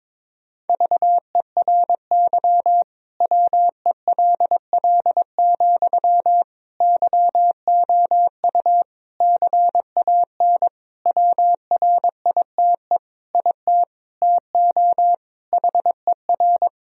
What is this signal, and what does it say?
Morse code, heard as a keyed tone, about 22 wpm, 700 Hz. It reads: VERY WELL, YOU CAN WRITE IT TO HER
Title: Sentences from Top 1000 Words - Review Part 4 - 22wpm